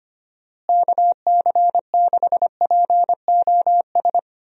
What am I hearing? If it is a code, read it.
Morse code, heard as a keyed tone, about 25 wpm, 700 Hz. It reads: KC6POS
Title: USA Call Signs - 6 Characters - Repeated 2x - 25wpm